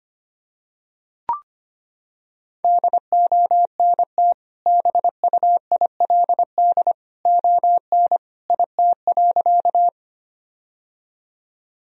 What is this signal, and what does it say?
Morse code, heard as a keyed tone, about 25 wpm, 700 Hz. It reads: DONT BUILD ON IT.
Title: Sentences from Top 1000 Words - Review Part 1 - 25wpm